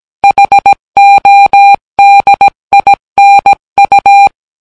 Morse code, heard as a keyed tone, about 17 wpm, 800 Hz. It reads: HODINU